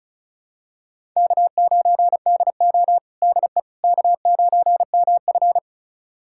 Morse code, heard as a keyed tone, about 36 wpm, 700 Hz. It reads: K9DO DE K9MF
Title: USA Callsign DE Callsign - 4 Characters - 35wpm